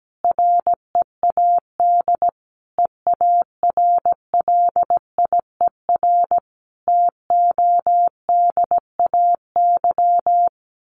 Morse code, heard as a keyed tone, about 17 wpm, 700 Hz. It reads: READ EARLIER TODAY
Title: A Sound Of Thunder 17wpm